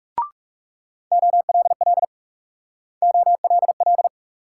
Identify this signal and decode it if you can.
Morse code, heard as a keyed tone, about 37 wpm, 700 Hz. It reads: OLL OLL
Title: Speed-Racing Multi-letter Phonemes - 30wpm